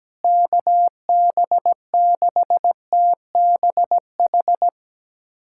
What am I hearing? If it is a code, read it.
Morse code, heard as a keyed tone, about 17 wpm, 700 Hz. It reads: KB6TBH
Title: USA Callsign DE Callsign - 6 Characters - 17wpm